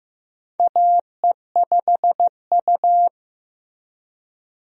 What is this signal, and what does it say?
Morse code, heard as a keyed tone, about 15 wpm, 700 Hz. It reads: AE5U
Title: USA Call Signs - 4 Characters - Repeated 2x - 15wpm